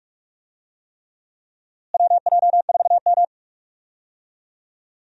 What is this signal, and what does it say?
Morse code, heard as a keyed tone, about 47 wpm, 700 Hz. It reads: WJ4M
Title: USA Call Signs - 4 Characters - Repeated 3x - 45wpm